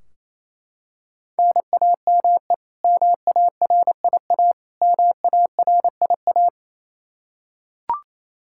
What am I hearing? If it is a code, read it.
Morse code, heard as a keyed tone, about 28 wpm, 700 Hz. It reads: NAME MARIA MARIA
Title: Practice QSO - Exchange 1 - Beginning - 28wpm